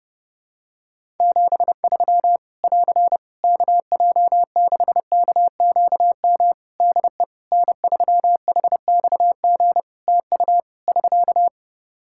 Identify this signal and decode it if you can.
Morse code, heard as a keyed tone, about 30 wpm, 700 Hz. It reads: 73 <AR> KJ6KQM DE N3HXG TU <SK>
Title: Practice QSO - Exchange 4 - Ending - 30wpm